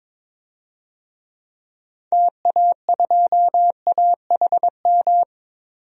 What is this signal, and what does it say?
Morse code, heard as a keyed tone, about 22 wpm, 700 Hz. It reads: TA2AHM
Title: DXCC-100 Call Signs - 6 Characters - 22wpm